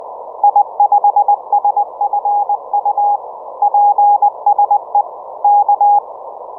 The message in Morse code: I5SFU PSE K